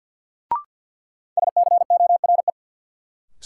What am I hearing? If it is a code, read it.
Morse code, heard as a keyed tone, about 52 wpm, 700 Hz. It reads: SCORE
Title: Sentences from Top 1600 Words - New Word Drill - 50wpm